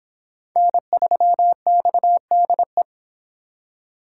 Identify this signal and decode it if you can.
Morse code, heard as a keyed tone, about 29 wpm, 700 Hz. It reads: N3XDE